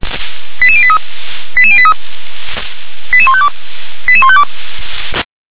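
Five-tone selective call (ZVEI), heard as quick runs of five tones, tones of about 70 ms. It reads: 88083 88083 88152 88152
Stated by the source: Five-tone ZVEI selective call received on an SDR.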